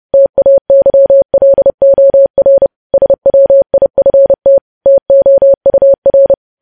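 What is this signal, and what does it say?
Morse code, heard as a keyed tone, about 30 wpm, 550 Hz. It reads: TAYLOR SWIFT TOUR
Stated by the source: Morse code practice of news headlines at 30.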